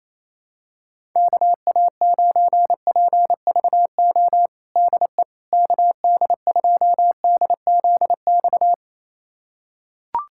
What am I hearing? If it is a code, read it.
Morse code, heard as a keyed tone, about 28 wpm, 700 Hz. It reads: KA9PVO DE KD2DZX
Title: USA Callsign DE Callsign - 6 Characters - 28wpm